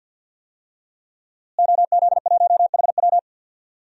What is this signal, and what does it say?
Morse code, heard as a keyed tone, about 50 wpm, 700 Hz. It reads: OZ1HW
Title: DXCC-100 Call Signs - 5 Characters - Repeated 3x - 50wpm